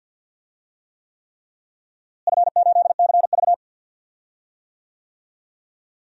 Morse code, heard as a keyed tone, about 47 wpm, 700 Hz. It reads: F8CV
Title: DXCC-100 Call Signs - 4 Characters - Repeated 3x - 50wpm